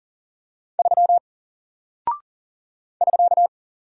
Morse code, heard as a keyed tone, about 40 wpm, 700 Hz. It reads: 3 <SK>
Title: Single Character (Letter:Number:Punctuation:Prosign) 40wpm